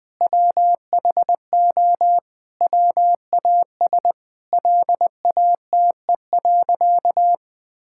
Morse code, heard as a keyed tone, about 20 wpm, 700 Hz. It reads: WHO WAS LATE.
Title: Sentences with 2-Word Phrases - 20wpm